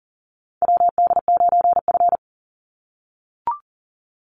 Morse code, heard as a keyed tone, about 40 wpm, 700 Hz. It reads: WD9F
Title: NA SOTA Chaser Callsigns 40wpm